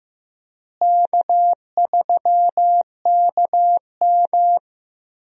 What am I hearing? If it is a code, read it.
Morse code, heard as a keyed tone, about 16 wpm, 700 Hz. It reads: K3KM
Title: USA Call Signs - 4 Characters - Repeated 3x - 15wpm